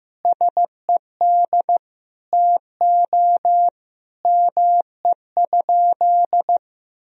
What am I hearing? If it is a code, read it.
Morse code, heard as a keyed tone, about 15 wpm, 700 Hz. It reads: SED TO ME?
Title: Sentences from Top 200 Words and Abbreviations 15wpm